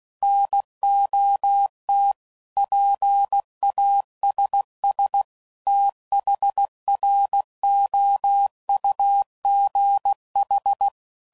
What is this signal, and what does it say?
Morse code, heard as a keyed tone, about 16 wpm, 800 Hz. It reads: NOT PASS THROUGH
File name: CW Code Groups @15WPM/Technical Subjects